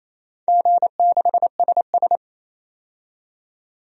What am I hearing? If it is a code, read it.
Morse code, heard as a keyed tone, about 28 wpm, 700 Hz. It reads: G6SS
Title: DXCC-100 Call Signs - 4 Characters - 28wpm